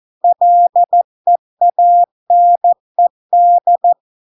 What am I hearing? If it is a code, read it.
Morse code, heard as a keyed tone, about 14 wpm, 700 Hz. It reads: LEANED